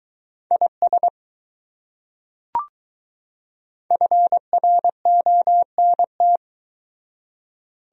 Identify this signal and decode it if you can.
Morse code, heard as a keyed tone, about 20 wpm, 700 Hz. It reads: IS FRONT